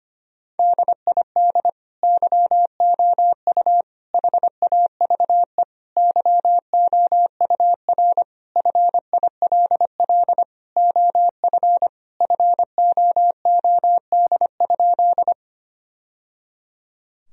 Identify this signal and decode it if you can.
Morse code, heard as a keyed tone, about 25 wpm, 700 Hz. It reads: DID YOU HAVE YOUR FILL OF FOOD?
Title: Sentences from Top 1000 Words - Review Part 3 - 25wpm